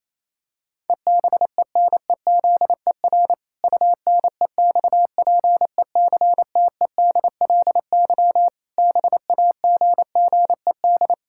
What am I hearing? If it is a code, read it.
Morse code, heard as a keyed tone, about 28 wpm, 700 Hz. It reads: EBENEZER UNEXPECTEDLY BAGGED
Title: Pangram Challenge v2 - 28wpm